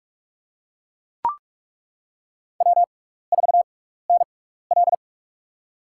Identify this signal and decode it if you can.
Morse code, heard as a keyed tone, about 43 wpm, spaced as for 15 wpm, 700 Hz. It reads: W4NL
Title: USA Call Signs - 4 Characters - 6x Character Spacing - 45wpm